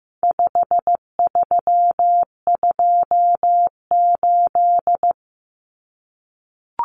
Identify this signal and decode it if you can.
Morse code, heard as a keyed tone, about 15 wpm, 700 Hz. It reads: 5328
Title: Numbers - 4-Digit 15wpm